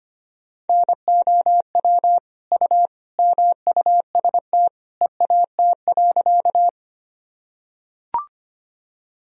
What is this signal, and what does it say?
Morse code, heard as a keyed tone, about 25 wpm, 700 Hz. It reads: NOW U MUST EAT.
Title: Sentences from Top 300 Words and Abbreviations 25wpm